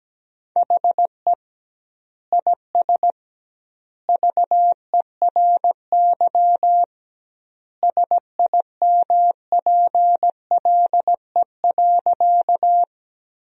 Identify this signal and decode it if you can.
Morse code, heard as a keyed tone, about 17 wpm, 700 Hz. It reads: HE IS VERY SIMPLE.
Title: 2x Word-Spacing - Sentences from Top 400 Words - 17wpm